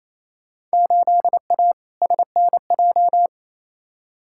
Morse code, heard as a keyed tone, about 28 wpm, 700 Hz. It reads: 8A SNJ
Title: Field Day Exchanges Encore - 28wpm